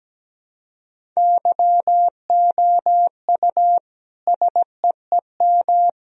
Morse code, heard as a keyed tone, about 17 wpm, 700 Hz. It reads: YOU SEEM
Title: Sentences from Top 1000 Words - Review Part 1 - 17wpm